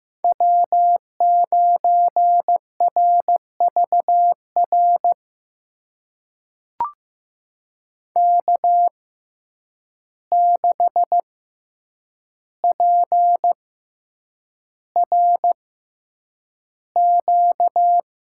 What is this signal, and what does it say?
Morse code, heard as a keyed tone, about 15 wpm, spaced as for 8 wpm, 700 Hz. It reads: W9RVR K6PRQ
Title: USA Call Signs - 5 Characters - 6x Character Spacing - 15wpm